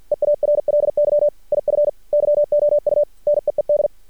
Morse code, heard as a keyed tone, about 53 wpm, 590 Hz. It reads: EARLY IF YOU NEED
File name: archived / CRSnet